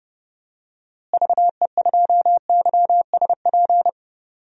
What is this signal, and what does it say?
Morse code, heard as a keyed tone, about 30 wpm, 700 Hz. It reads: VE2YSP
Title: DXCC-100 Call Signs - 6 Characters - Repeated 2x - 30wpm